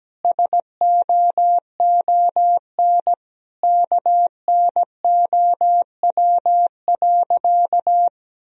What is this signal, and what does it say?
Morse code, heard as a keyed tone, about 17 wpm, 700 Hz. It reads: SOON KNOW.
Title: Sentences from Top 400 Words and Abbreviations 17wpm